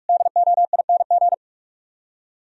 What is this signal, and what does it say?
Morse code, heard as a keyed tone, about 64 wpm, spaced as for 54 wpm, 700 Hz. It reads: DOING
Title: Speed-Racing - Sets of 4 Words 45wpm